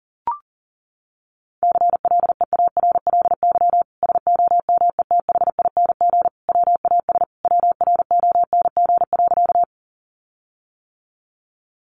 Morse code, heard as a keyed tone, about 40 wpm, 700 Hz. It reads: CLEARLY SOMETHING WAS WRONG.
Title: Sentences from Top 1300 Words 40wpm